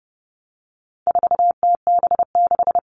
Morse code, heard as a keyed tone, about 30 wpm, 700 Hz. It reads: 4TB6
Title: All Cut Numbers - 4-Digit 30wpm